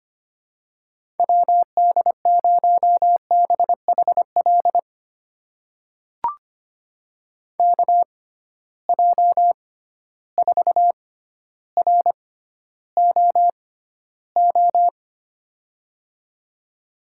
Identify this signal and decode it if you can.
Morse code, heard as a keyed tone, about 25 wpm, spaced as for 13 wpm, 700 Hz. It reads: WD0BHL KJ4ROO
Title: USA Call Signs - 6 Characters - 6x Character Spacing - 25wpm